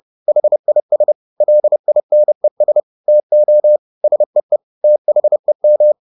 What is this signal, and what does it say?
Morse code, heard as a keyed tone, about 30 wpm, 600 Hz. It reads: HIS LINES TO SEE THEM
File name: the old man and the sea / the old man and the sea 1 30wpm